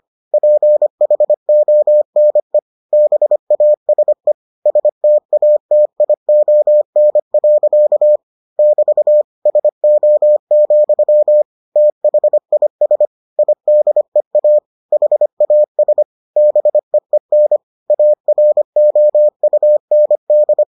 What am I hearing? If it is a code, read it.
Morse code, heard as a keyed tone, about 25 wpm, 600 Hz. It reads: PHONE BASE STATION. <BT> SO, THIS IDEA HAS BEEN AROUND